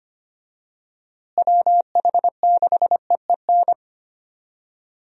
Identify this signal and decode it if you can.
Morse code, heard as a keyed tone, about 25 wpm, 700 Hz. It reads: WH6EEN
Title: USA Call Signs - 6 Characters - Repeated 2x - 25wpm